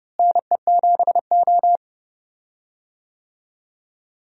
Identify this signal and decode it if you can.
Morse code, heard as a keyed tone, about 30 wpm, 700 Hz. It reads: NE7O
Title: USA Call Signs - 4 Characters - Repeated 2x - 30wpm